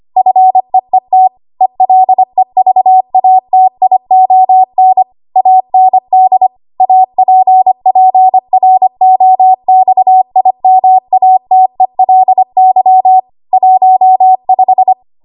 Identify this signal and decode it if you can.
Morse code, heard as a keyed tone, about 25 wpm, 750 Hz. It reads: FEET ELEVATION AND APPROXIMATELY 15
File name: Archive / 25 WPM / 25WPM